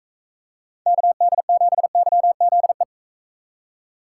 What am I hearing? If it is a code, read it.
Morse code, heard as a keyed tone, about 44 wpm, 700 Hz. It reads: KD7YZE